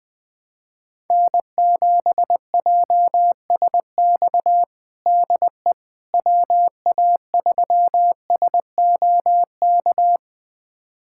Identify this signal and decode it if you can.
Morse code, heard as a keyed tone, about 20 wpm, 700 Hz. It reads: N7JSX DE WA3SOK
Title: USA Callsign DE Callsign 20wpm